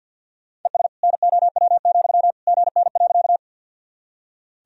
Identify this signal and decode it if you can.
Morse code, heard as a keyed tone, about 47 wpm, 700 Hz. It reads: ES NOW, GN.
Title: Sentences from Top 300 Words and Abbreviations 50wpm